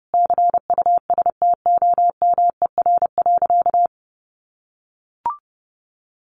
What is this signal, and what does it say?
Morse code, heard as a keyed tone, about 28 wpm, 700 Hz. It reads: CUSTOMER.
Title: Sentences from Top 1100 Words 30wpm